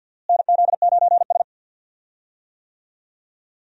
Morse code, heard as a keyed tone, about 51 wpm, 700 Hz. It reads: NZ9S
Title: USA Call Signs - 4 Characters - Repeated 3x - 50wpm